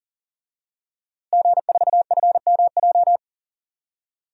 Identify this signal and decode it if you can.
Morse code, heard as a keyed tone, about 37 wpm, 700 Hz. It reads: G4FMJ
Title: DXCC-100 Call Signs - 5 Characters - Repeated 3x - 40wpm